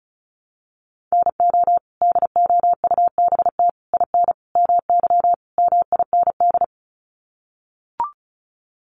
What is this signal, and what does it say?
Morse code, heard as a keyed tone, about 34 wpm, 700 Hz. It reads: NO DOUBT IN MY MIND
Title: Sets of 5 Words Encore2 35wpm